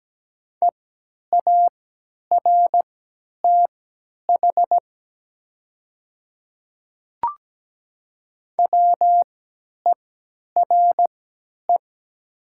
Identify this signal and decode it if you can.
Morse code, heard as a keyed tone, about 17 wpm, spaced as for 10 wpm, 700 Hz. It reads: EARTH WERE
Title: 3x Character Spacing - Top 200 Words - 17wpm